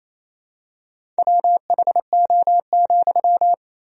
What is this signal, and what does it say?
Morse code, heard as a keyed tone, about 28 wpm, 700 Hz. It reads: WHO,
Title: Sentences from Top 100 Words and Abbreviations 28wpm